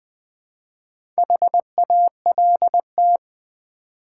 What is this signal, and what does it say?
Morse code, heard as a keyed tone, about 20 wpm, 700 Hz. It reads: HALT
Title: Build A Word - 2 Letters At A Time - 4-Letter Words - 20wpm